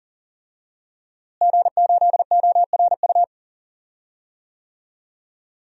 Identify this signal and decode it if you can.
Morse code, heard as a keyed tone, about 40 wpm, 700 Hz. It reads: G8ORU
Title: DXCC-100 Call Signs - 5 Characters - Repeated 2x - 40wpm